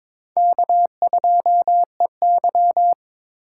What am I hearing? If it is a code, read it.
Morse code, heard as a keyed tone, about 26 wpm, 700 Hz. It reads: K2EY
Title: Speed Racing - USA 1x2 Call Signs 22wpm